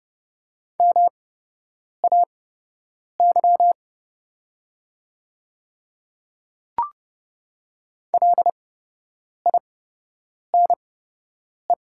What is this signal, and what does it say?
Morse code, heard as a keyed tone, about 29 wpm, spaced as for 8 wpm, 700 Hz. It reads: MAY LINE
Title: 8x Character Spacing - Top 200 Words - 30wpm